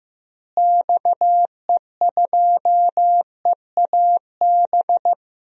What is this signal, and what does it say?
Morse code, heard as a keyed tone, about 15 wpm, 700 Hz. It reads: XE2EAB
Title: DXCC-100 Call Signs - 6 Characters - Repeated 3x - 15wpm